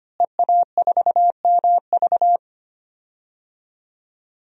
Morse code, heard as a keyed tone, about 25 wpm, 700 Hz. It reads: EA4MV
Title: DXCC-100 Call Signs - 5 Characters - Repeated 3x - 25wpm